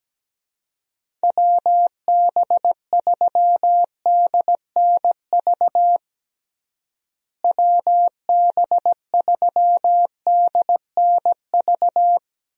Morse code, heard as a keyed tone, about 17 wpm, 700 Hz. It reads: WB3DNV WB3DNV
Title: USA Call Signs - 6 Characters - Repeated 2x - 17wpm